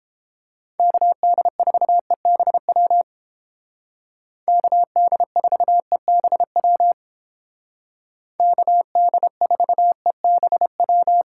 Morse code, heard as a keyed tone, about 30 wpm, 700 Hz. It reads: KD4EBW KD4EBW KD4EBW